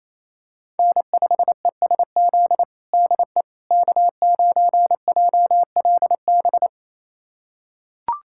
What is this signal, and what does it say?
Morse code, heard as a keyed tone, about 28 wpm, 700 Hz. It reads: N5ESZ DE K9JLB
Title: USA Callsign DE Callsign - 5 Characters - 28wpm